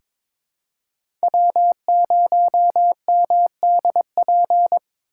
Morse code, heard as a keyed tone, about 22 wpm, 700 Hz. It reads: W0MDP
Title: USA Call Signs - 4x Character Spacing - 22wpm